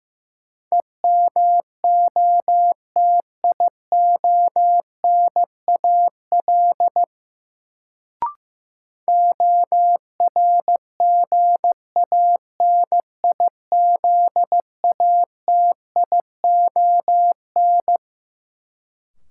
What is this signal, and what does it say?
Morse code, heard as a keyed tone, about 15 wpm, 700 Hz. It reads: EMOTIONAL ORGANIZATION
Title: Sentences from Top 1000 Words - Review Part 5 - New Word Drill - 15wpm